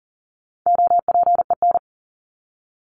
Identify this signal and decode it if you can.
Morse code, heard as a keyed tone, about 40 wpm, 700 Hz. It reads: OPEN
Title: Top 300 Words 40wpm